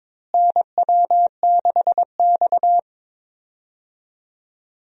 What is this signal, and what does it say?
Morse code, heard as a keyed tone, about 22 wpm, 700 Hz. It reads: NW6X
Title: USA Call Signs - 4 Characters - Repeated 2x - 22wpm